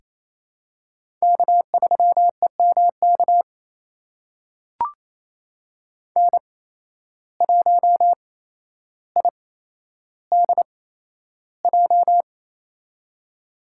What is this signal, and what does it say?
Morse code, heard as a keyed tone, about 28 wpm, spaced as for 12 wpm, 700 Hz. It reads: K3EMK N1IDJ
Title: USA Call Signs - 5 Characters - 8x Character Spacing - 28wpm